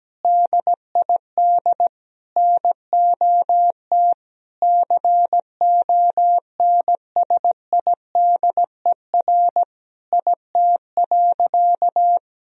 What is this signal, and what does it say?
Morse code, heard as a keyed tone, about 17 wpm, 700 Hz. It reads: DID NOT CONSIDER IT.
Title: Sentences from Top 1000 Words - Review Part 4 - 17wpm